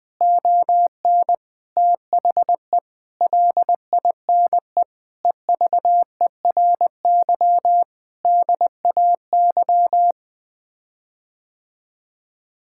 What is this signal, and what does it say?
Morse code, heard as a keyed tone, about 20 wpm, 700 Hz. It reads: ON THE LINE EVERY DAY
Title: Sets of 8 Words - 20wpm